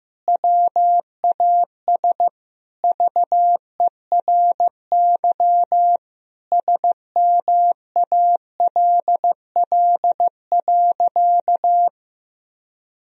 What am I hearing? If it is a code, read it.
Morse code, heard as a keyed tone, about 15 wpm, 700 Hz. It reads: WAS VERY SMALL.
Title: Sentences from Top 1000 Words - Review Part 1 - 15wpm